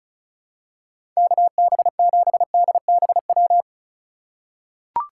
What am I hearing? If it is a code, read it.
Morse code, heard as a keyed tone, about 36 wpm, 700 Hz. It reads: KB7DBW
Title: USA Call Signs - 6 Characters - 4x Character Spacing - 35wpm